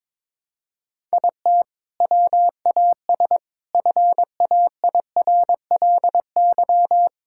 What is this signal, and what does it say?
Morse code, heard as a keyed tone, about 22 wpm, 700 Hz. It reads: IT WAS FAIRLY
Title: Sentences from Top 1600 Words 22wpm